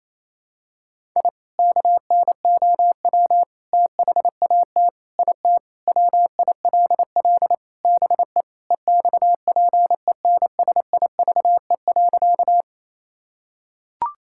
Morse code, heard as a keyed tone, about 28 wpm, 700 Hz. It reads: I KNOW THAT IT WILL BE EXPENSIVE.
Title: Sentences from Top 1000 Words - Review Part 6 - 28wpm